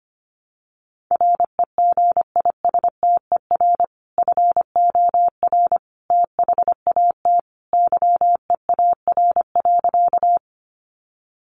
Morse code, heard as a keyed tone, about 25 wpm, 700 Hz. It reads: REGISTER FOR THAT YEAR.
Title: Sentences from Top 1200 Words 25wpm